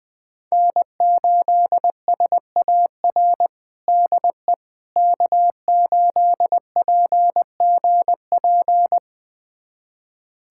Morse code, heard as a keyed tone, about 21 wpm, 700 Hz. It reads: N8SAR DE K8PGP
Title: USA Callsign DE Callsign - 5 Characters - 20wpm